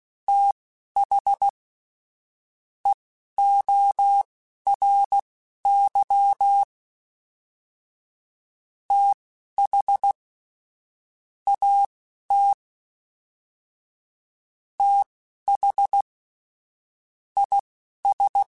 Morse code, heard as a keyed tone, about 16 wpm, spaced as for 11 wpm, 800 Hz. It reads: TH EORY TH AT TH IS
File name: TH Runs sp / TH Runs sp.mp3s / TH Runs sp @10